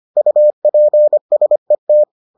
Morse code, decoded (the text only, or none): UPSET